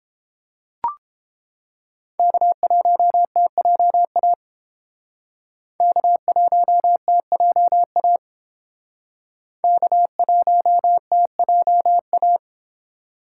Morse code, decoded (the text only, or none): K1TJA K1TJA K1TJA